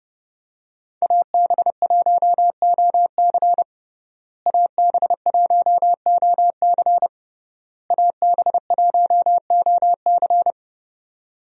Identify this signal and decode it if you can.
Morse code, heard as a keyed tone, about 30 wpm, 700 Hz. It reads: AB1OC AB1OC AB1OC
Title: USA Call Signs - 5 Characters - Repeated 3x - 30wpm